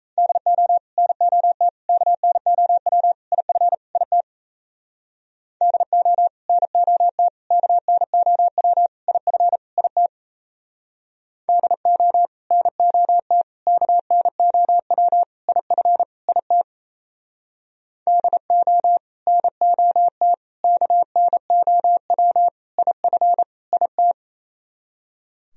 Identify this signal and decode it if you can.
Morse code, heard as a keyed tone, about 34 wpm, 700 Hz. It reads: DO NOT KNOW IF IT DO NOT KNOW IF IT DO NOT KNOW IF IT DO NOT KNOW IF IT
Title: Speed-Racing - Sets of 5 Words 28wpm